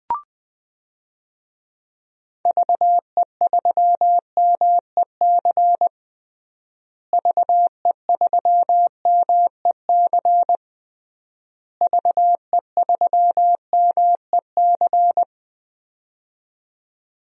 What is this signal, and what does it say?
Morse code, heard as a keyed tone, about 20 wpm, 700 Hz. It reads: VE3MEC VE3MEC VE3MEC
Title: DXCC-100 Call Signs - 6 Characters - Repeated 3x - 20wpm